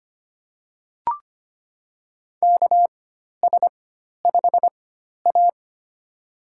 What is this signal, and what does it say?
Morse code, heard as a keyed tone, about 25 wpm, spaced as for 12 wpm, 700 Hz. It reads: KS5A
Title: USA Call Signs - 4 Characters - 4x Character Spacing - 25wpm